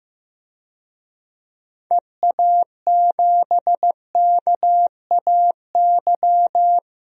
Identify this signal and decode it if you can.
Morse code, heard as a keyed tone, about 15 wpm, 700 Hz. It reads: EA7KAY
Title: DXCC-100 Call Signs - 6 Characters - 15wpm